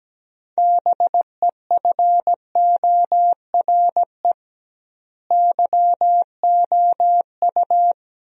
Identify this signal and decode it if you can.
Morse code, heard as a keyed tone, about 17 wpm, 700 Hz. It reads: BEFORE YOU
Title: 2x Word-Spacing - Sentences from Top 400 Words - 17wpm